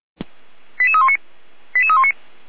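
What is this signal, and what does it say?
Five-tone selective call (ZVEI), heard as quick runs of five tones, tones of about 70 ms. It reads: 88319 88319
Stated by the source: ZVEI five-tone call received by SDR for feuerwehr-orlamuende, a fire brigade at Orlamünde.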